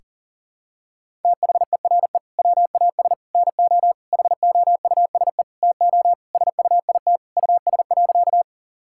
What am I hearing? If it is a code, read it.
Morse code, heard as a keyed tone, about 40 wpm, 700 Hz. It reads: THERE WAS NO HOUSE TO SUIT US.
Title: Sentences from Top 1600 Words 40wpm